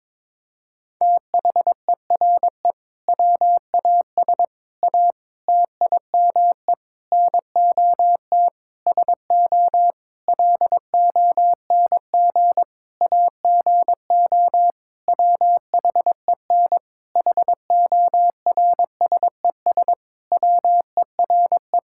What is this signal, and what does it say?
Morse code, heard as a keyed tone, about 22 wpm, 700 Hz. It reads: THERE WAS A TIME NOT SO LONG AGO WHEN HORSES WERE
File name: practice / 22wpm / Sentences with 8-Word Phrases - 22wpm